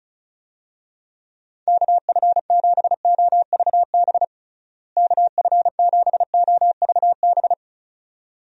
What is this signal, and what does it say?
Morse code, heard as a keyed tone, about 36 wpm, 700 Hz. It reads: KF7OVB KF7OVB
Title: USA Call Signs - 6 Characters - Repeated 2x - 35wpm